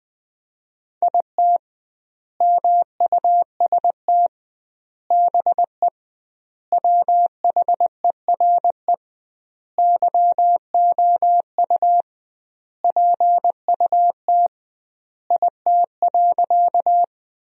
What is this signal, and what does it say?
Morse code, heard as a keyed tone, about 20 wpm, 700 Hz. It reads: IT MUST BE WHERE YOU PUT IT.
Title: 2x Word-Spacing - Sentences from Top 200 Words - 20wpm